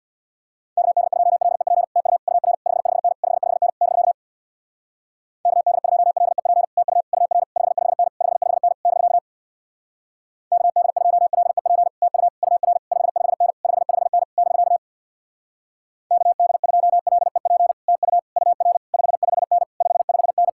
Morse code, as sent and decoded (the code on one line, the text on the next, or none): -.- -.. ..--- .-.. .--. / - ..- / ..- .-. / ..... ..... -. / ..... ..... -. / -...-.- / -.- -.. ..--- .-.. .--. / - ..- / ..- .-. / ..... ..... -. / ..... ..... -. / -...-.- / -.- -.. ..--- .-.. .--. / - ..- / ..- .-. / ..... ..... -. / ..... ..... -. / -...-.- / -.- -.. ..--- .-.. .--. / - ..- / ..- .-. / ..... ..... -. / ..... ..... -.
KD2LEP TU UR 55N 55N <BK> KD2LEP TU UR 55N 55N <BK> KD2LEP TU UR 55N 55N <BK> KD2LEP TU UR 55N 55N